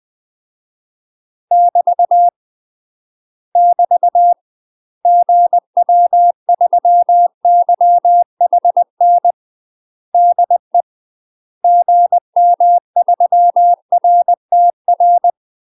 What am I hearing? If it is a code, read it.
Morse code, heard as a keyed tone, about 20 wpm, 700 Hz. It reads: <BT> <BT> GW3YHN DE GM3RTR
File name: UK MASTER QSOs TRAINING / 20WPM / UK MASTER QSOs 0074